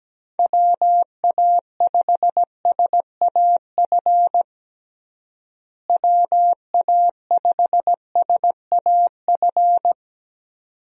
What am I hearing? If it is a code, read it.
Morse code, heard as a keyed tone, about 17 wpm, 700 Hz. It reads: WA5SAF WA5SAF
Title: USA Call Signs - 6 Characters - Repeated 2x - 17wpm